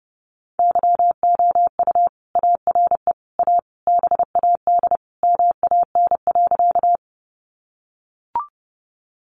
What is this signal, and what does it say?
Morse code, heard as a keyed tone, about 30 wpm, 700 Hz. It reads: YOU ARE A BAD MAN.
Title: Sentences from Top 700 Words 30wpm